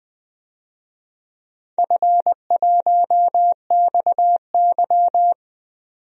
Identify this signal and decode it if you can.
Morse code, heard as a keyed tone, about 20 wpm, 700 Hz. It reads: F1XY
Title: DXCC-100 Call Signs - 4 Characters - 20wpm